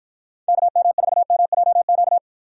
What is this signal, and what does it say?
Morse code, heard as a keyed tone, about 51 wpm, 700 Hz. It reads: KM4MJQ